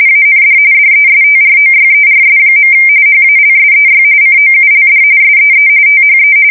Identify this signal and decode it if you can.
RTTY, about 45.45 baud. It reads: AND DESTROYERS. HALSEY COULD SEE THE
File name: fsk170 / USS-New-Jersey